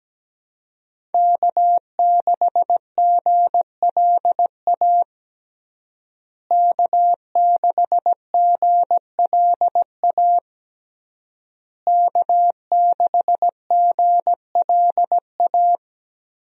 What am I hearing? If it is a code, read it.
Morse code, heard as a keyed tone, about 17 wpm, 700 Hz. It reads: K6GLA K6GLA K6GLA
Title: USA Call Signs - 5 Characters - Repeated 3x - 17wpm